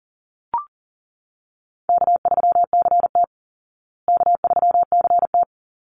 Morse code, heard as a keyed tone, about 40 wpm, 700 Hz. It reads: K3CT K3CT
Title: Call Signs Repeated 3x - 40wpm